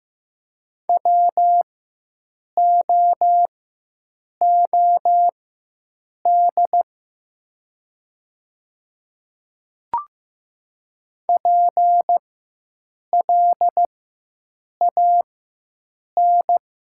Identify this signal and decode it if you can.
Morse code, heard as a keyed tone, about 15 wpm, spaced as for 7 wpm, 700 Hz. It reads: WOOD PLAN
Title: 4x Character Spacing - Top 500 Words - 15wpm